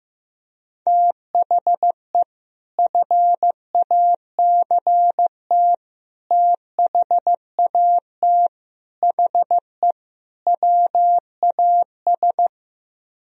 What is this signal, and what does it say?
Morse code, heard as a keyed tone, about 15 wpm, 700 Hz. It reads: THE FACT THAT HE WAS
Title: Speed-Racing - Sets of 5 Words 15wpm